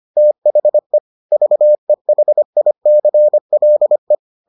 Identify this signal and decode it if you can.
Morse code, heard as a keyed tone, about 25 wpm, 600 Hz. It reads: THE VEHICLE